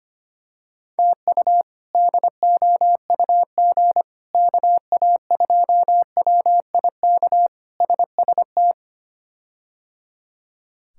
Morse code, heard as a keyed tone, about 25 wpm, 700 Hz. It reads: TU DOUG KA2WIK SST
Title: K1USN SST QSO Practice - 25wpm